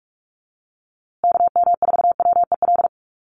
Morse code, heard as a keyed tone, about 44 wpm, 700 Hz. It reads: KM4WEL
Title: USA Call Signs Repeated 2x 45wpm